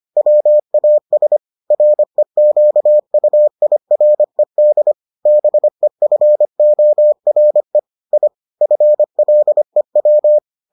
Morse code, heard as a keyed tone, about 25 wpm, 600 Hz. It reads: WAS REQUIRED BEFORE I FLEW